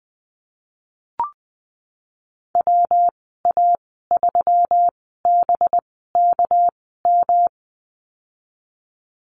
Morse code, heard as a keyed tone, about 20 wpm, spaced as for 14 wpm, 700 Hz. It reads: WA3BKM
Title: 231 Lesson 37g - Letter x - Call Signs 20wpm